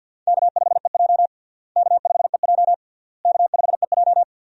Morse code, heard as a keyed tone, about 52 wpm, 700 Hz. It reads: K5EJ K5EJ K5EJ
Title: USA Call Signs - 4 Characters - Repeated 3x - 50wpm